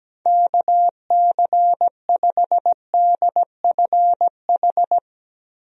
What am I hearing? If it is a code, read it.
Morse code, heard as a keyed tone, about 17 wpm, 700 Hz. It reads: KC5DFH
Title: USA Call Signs - 6 Characters - Repeated 3x - 17wpm